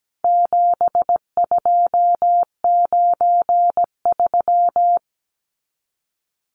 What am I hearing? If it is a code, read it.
Morse code, heard as a keyed tone, about 17 wpm, 700 Hz. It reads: 7293
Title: Numbers - 4-Digit 17wpm